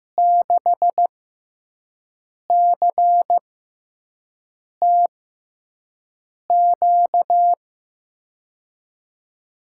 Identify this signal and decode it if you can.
Morse code, heard as a keyed tone, about 15 wpm, spaced as for 5 wpm, 700 Hz. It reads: 6CTQ
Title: USA Call Signs - 5 Characters - 6x Character Spacing - 15wpm